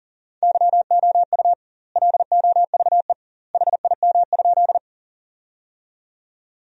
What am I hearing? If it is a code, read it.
Morse code, heard as a keyed tone, about 40 wpm, 700 Hz. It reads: YOU LOVE HIM?
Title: 2x Word-Spacing - Sentences from Top 400 Words - 40wpm